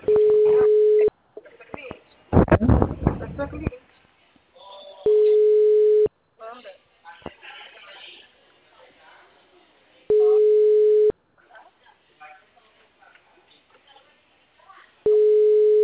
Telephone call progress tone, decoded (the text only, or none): ringback tone